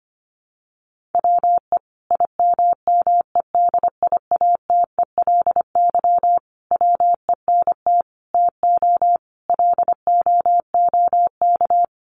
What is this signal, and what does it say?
Morse code, heard as a keyed tone, about 25 wpm, 700 Hz. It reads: WE IMMEDIATELY WENT TO LOOK
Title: Sentences from Top 1100 Words 25wpm